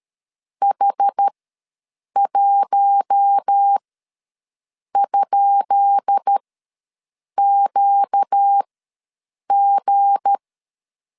Morse code, heard as a keyed tone, about 13 wpm, spaced as for 6 wpm, 800 Hz. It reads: H1?QG
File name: LEZIONI / 10 SIGNS / 10B INT ws 12wpm 3527